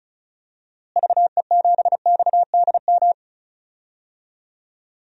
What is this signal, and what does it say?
Morse code, heard as a keyed tone, about 36 wpm, 700 Hz. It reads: VE7XDM
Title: DXCC-100 Call Signs - 6 Characters - Repeated 3x - 35wpm